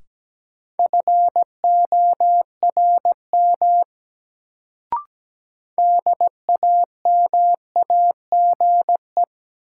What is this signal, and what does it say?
Morse code, heard as a keyed tone, about 17 wpm, 700 Hz. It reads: FORM DAMAGE
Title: Sentences from Top 1100 Words - New Word Drill Alt - 17wpm